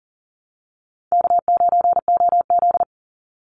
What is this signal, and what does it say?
Morse code, heard as a keyed tone, about 40 wpm, 700 Hz. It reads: K9OZ
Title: Easy Call Signs Repeated 2x - 40wpm